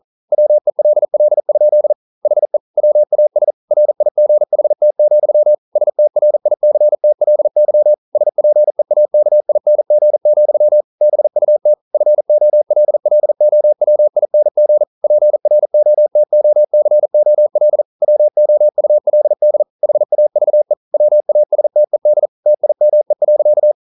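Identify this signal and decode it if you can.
Morse code, heard as a keyed tone, about 42 wpm, 600 Hz. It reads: WELL? HE WAS RIGHT, STRICTLY SPEAKING, BUT FOLLOWING PROTOCOL WOULD HAVE WASTED TIME.
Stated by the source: Morse code practice at 41 wpm, from CW med Gnister.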